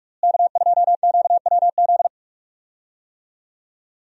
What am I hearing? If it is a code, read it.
Morse code, heard as a keyed tone, about 41 wpm, 700 Hz. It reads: K2QWZ
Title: USA Call Signs - 5 Characters - Repeated 2x - 45wpm